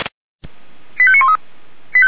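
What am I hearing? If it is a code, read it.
Five-tone selective call (ZVEI), heard as quick runs of five tones, tones of about 70 ms. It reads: 85913
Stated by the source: ZVEI five-tone call received by SDR.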